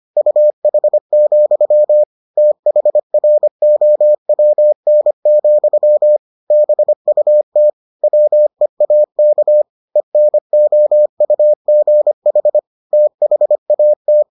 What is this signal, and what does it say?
Morse code, heard as a keyed tone, about 25 wpm, 600 Hz. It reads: UH, THROWN, BUT WEAK ENOUGH THAT